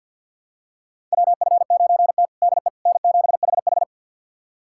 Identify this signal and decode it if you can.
Morse code, heard as a keyed tone, about 46 wpm, 700 Hz. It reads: WF9T DE N7HH
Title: USA Callsign DE Callsign - 4 Characters - 50wpm